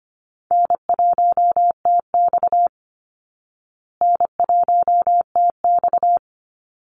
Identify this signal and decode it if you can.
Morse code, heard as a keyed tone, about 25 wpm, 700 Hz. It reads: N1TX N1TX
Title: Easy Call Signs Repeated 3x - 25wpm